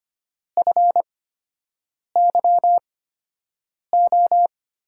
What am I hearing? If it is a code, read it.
Morse code, heard as a keyed tone, about 25 wpm, spaced as for 7 wpm, 700 Hz. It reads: FYO
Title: USA Call Signs - 6 Characters - 8x Character Spacing - 25wpm